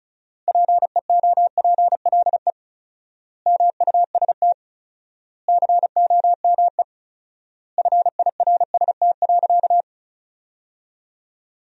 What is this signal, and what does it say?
Morse code, heard as a keyed tone, about 36 wpm, 700 Hz. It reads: PEOPLE MUST COME FIRST.
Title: 4x Word-Spacing - Sentences from Top 300 Words - 35wpm